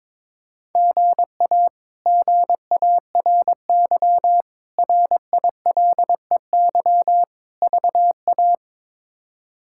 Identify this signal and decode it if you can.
Morse code, heard as a keyed tone, about 21 wpm, 700 Hz. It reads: GA GARY RILEY VA
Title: K1USN SST QSO Practice - 22wpm